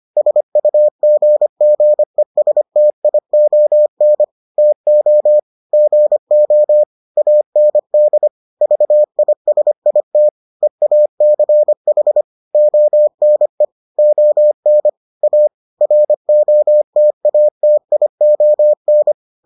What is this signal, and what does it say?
Morse code, heard as a keyed tone, about 25 wpm, 600 Hz. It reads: SUGGESTION TO GO AND VISIT EACH ONE ON A ROTATION